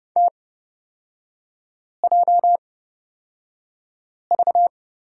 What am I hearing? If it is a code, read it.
Morse code, heard as a keyed tone, about 30 wpm, spaced as for 5 wpm, 690 Hz. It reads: TJV
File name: Lesson 38e - Letter q - Three Characters 30wpm